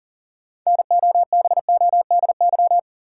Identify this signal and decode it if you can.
Morse code, heard as a keyed tone, about 40 wpm, 700 Hz. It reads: NOBODY
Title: Sentences from Top 1600 Words - New Word Drill Alt - 40wpm